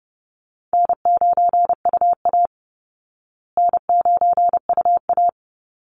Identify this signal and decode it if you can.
Morse code, heard as a keyed tone, about 30 wpm, 700 Hz. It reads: N9UA N9UA
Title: Easy Call Signs Repeated 2x - 30wpm